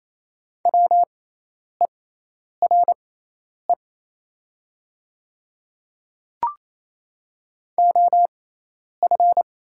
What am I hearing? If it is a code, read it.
Morse code, heard as a keyed tone, about 28 wpm, spaced as for 10 wpm, 700 Hz. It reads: WERE OF
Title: 6x Character Spacing - Top 100 Words - 28wpm